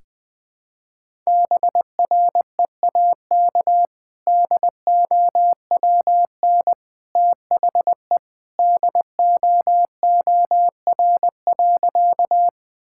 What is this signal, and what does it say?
Morse code, heard as a keyed tone, about 20 wpm, 700 Hz. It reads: BREAK DOWN THE DOOR.
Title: Sentences from Top 1000 Words - Review Part 6 - 20wpm